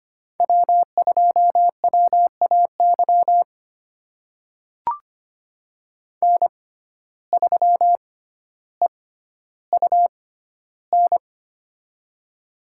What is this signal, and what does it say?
Morse code, heard as a keyed tone, about 25 wpm, spaced as for 13 wpm, 700 Hz. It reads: W2WAY N3EUN
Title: USA Call Signs - 5 Characters - 6x Character Spacing - 25wpm